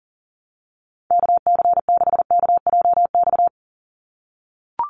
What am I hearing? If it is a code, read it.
Morse code, heard as a keyed tone, about 40 wpm, 700 Hz. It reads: KC6KJX
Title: USA Call Signs 40wpm